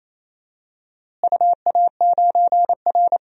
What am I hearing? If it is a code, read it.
Morse code, heard as a keyed tone, about 28 wpm, 700 Hz. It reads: UA9R
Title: DXCC-100 Call Signs - 4 Characters - Repeated 3x - 28wpm